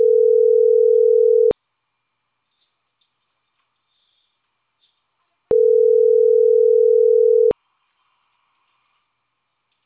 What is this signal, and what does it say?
Ringback tone (North America), 440 + 480 Hz.